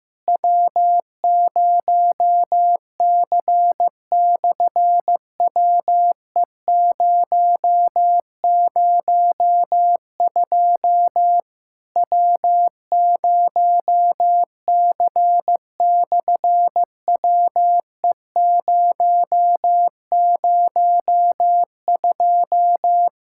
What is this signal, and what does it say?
Morse code, heard as a keyed tone, about 15 wpm, 700 Hz. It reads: W0C/WE002 W0C/WE002
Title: SOTA QSO Practice 15wpm